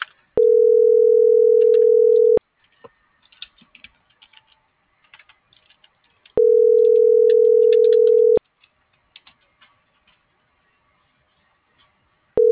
Ringback tone (North America), 440 + 480 Hz.